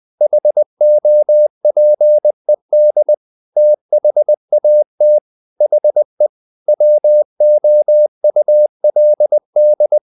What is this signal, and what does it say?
Morse code, heard as a keyed tone, about 20 wpm, 600 Hz. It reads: HOPED THAT HE WOULD
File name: the old man and the sea part 2 20wpm